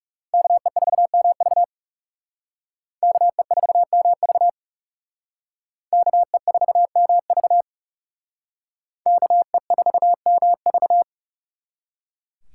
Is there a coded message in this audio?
Yes: Morse code from 0.3 to 11.0 s, heard as a keyed tone, about 39 wpm, 700 Hz. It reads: KE4MV KE4MV KE4MV KE4MV